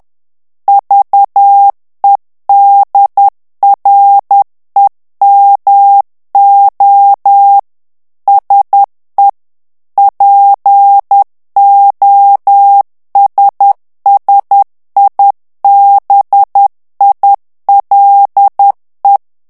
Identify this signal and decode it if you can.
Morse code, heard as a keyed tone, about 10 wpm, 800 Hz. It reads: VEDREMO SE POSSIBILE
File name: morse mp3 / ITALIEN texte clair 1 vitesse 10 mots minute